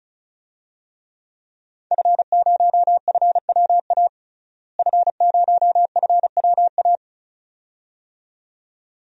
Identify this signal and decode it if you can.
Morse code, heard as a keyed tone, about 36 wpm, 700 Hz. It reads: F0FWA F0FWA
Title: DXCC-100 Call Signs - 5 Characters - Repeated 2x - 35wpm